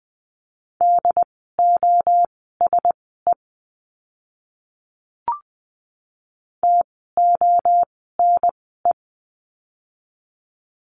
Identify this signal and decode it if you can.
Morse code, heard as a keyed tone, about 20 wpm, spaced as for 15 wpm, 700 Hz. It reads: DOSE TONE
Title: 068 Lesson 12f - Letter d - Words 20wpm